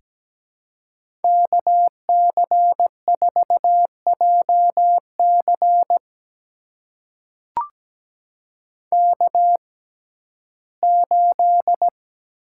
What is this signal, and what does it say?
Morse code, heard as a keyed tone, about 17 wpm, spaced as for 12 wpm, 700 Hz. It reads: KC4JC K8
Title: USA Call Signs - 6x Character Spacing - 17wpm